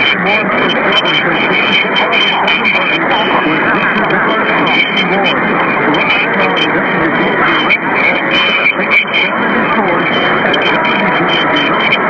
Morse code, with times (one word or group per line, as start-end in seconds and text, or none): none